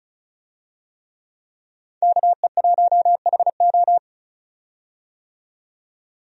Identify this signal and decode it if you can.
Morse code, heard as a keyed tone, about 36 wpm, 700 Hz. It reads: KE1HO
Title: USA Call Signs - 5 Characters - 35wpm